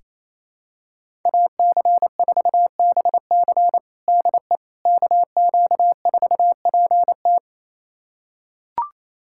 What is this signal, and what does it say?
Morse code, heard as a keyed tone, about 28 wpm, 700 Hz. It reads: AC4BC DE KQ4PT
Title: USA Callsign DE Callsign - 5 Characters - 28wpm